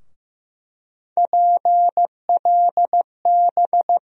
Morse code, heard as a keyed tone, about 15 wpm, 700 Hz. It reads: PLB
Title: USA Call Sign Postfixes - 3 Letters 15wpm